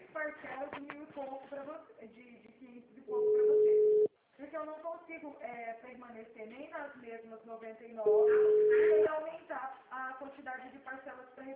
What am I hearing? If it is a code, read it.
Ringback tone (Europe, 425 Hz), with speech.
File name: out- -2016- - - .44396